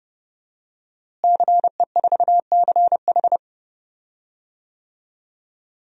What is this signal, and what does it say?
Morse code, heard as a keyed tone, about 32 wpm, 700 Hz. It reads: CE4CH
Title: DXCC-100 Call Signs - 5 Characters - Repeated 2x - 30wpm